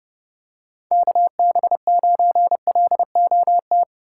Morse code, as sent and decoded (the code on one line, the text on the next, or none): -.- -... ----. .-.. --- -
KB9LOT